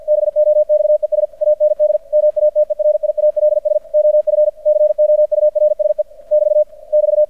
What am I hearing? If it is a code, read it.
Morse code, heard as a keyed tone, about 50 wpm, 600 Hz. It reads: BOXES AND MATERIALS OF COURSE <BT> <BT>
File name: archived / CRSnet